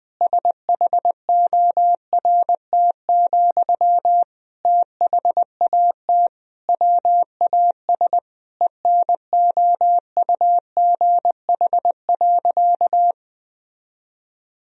Morse code, heard as a keyed tone, about 20 wpm, 700 Hz. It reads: SHORT, THAT WAS ENOUGH.
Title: Sentences from Top 1000 Words - Review Part 2 - 20wpm